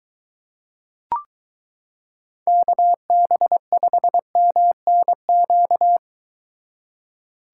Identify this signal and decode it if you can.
Morse code, heard as a keyed tone, about 22 wpm, 700 Hz. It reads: KB5MNQ